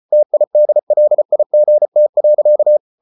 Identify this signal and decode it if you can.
Morse code, heard as a keyed tone, about 34 wpm, 600 Hz. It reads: TIDLIGT.
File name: Ferie QRQ 34wpm